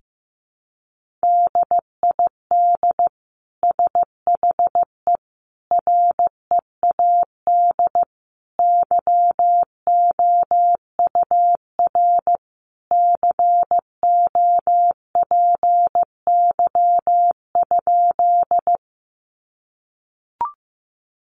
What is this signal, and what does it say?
Morse code, heard as a keyed tone, about 15 wpm, 700 Hz. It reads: DID SHE READ YOUR COPY?
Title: Sentences from Top 1400 Words 15wpm